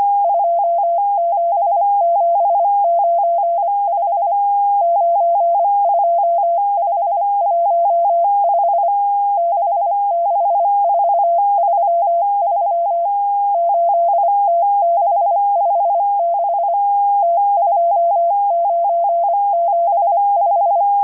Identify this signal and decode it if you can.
Morse code, heard as a keyed tone, about 25 wpm, 680 Hz, with radio noise. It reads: 27795 92515 66433 8T656 T2975